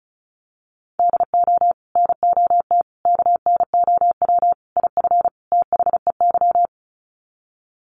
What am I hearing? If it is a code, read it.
Morse code, heard as a keyed tone, about 34 wpm, 700 Hz. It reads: DO NOT KNOW IF THEY
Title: Sets of 5 Words Encore 35wpm